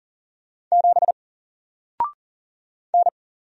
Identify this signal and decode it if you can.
Morse code, heard as a keyed tone, about 40 wpm, 700 Hz. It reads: 7 N
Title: Single Character (Letter:Number:Punctuation:Prosign) Alt 40wpm